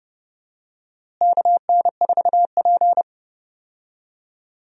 Morse code, heard as a keyed tone, about 31 wpm, 700 Hz. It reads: KN4P
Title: USA Call Signs - 4 Characters - Repeated 3x - 30wpm